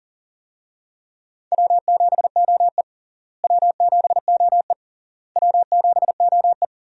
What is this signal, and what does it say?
Morse code, heard as a keyed tone, about 40 wpm, 700 Hz. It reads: W7OE W7OE W7OE
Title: USA Call Signs - 4 Characters - Repeated 3x - 40wpm